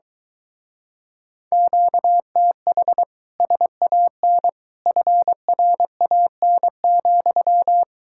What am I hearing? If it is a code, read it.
Morse code, heard as a keyed tone, about 22 wpm, 700 Hz. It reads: QTH SAN FRAN,